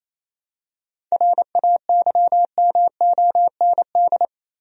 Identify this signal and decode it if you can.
Morse code, heard as a keyed tone, about 27 wpm, 700 Hz. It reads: RAYMOND
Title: USA Common Names - 28wpm